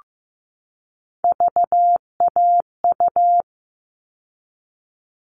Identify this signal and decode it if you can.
Morse code, heard as a keyed tone, about 15 wpm, 700 Hz. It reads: VAU
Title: All Cut Numbers - 3-Digit 15wpm